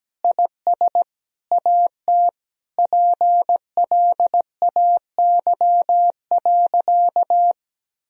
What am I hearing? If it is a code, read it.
Morse code, heard as a keyed tone, about 17 wpm, 700 Hz. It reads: IS AT PLAY.
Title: Sentences from Top 1000 Words - Review Part 4 - 17wpm